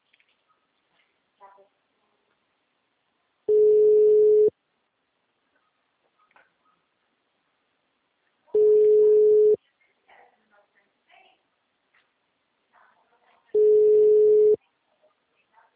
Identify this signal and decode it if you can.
Ringback tone (Europe, 425 Hz), with speech.